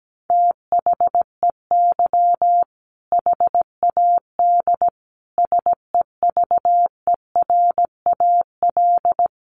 Morse code, heard as a keyed tone, about 17 wpm, 700 Hz. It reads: THEY HAD SEVERAL
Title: Sentences from Top 400 Words 17wpm